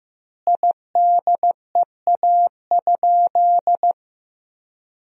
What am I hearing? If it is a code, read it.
Morse code, heard as a keyed tone, about 16 wpm, 700 Hz. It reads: IDEA?
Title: 2x Word-Spacing - Sentences from Top 400 Words - 15wpm